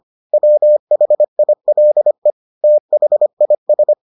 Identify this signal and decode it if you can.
Morse code, heard as a keyed tone, about 25 wpm, 600 Hz. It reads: WHILE THIS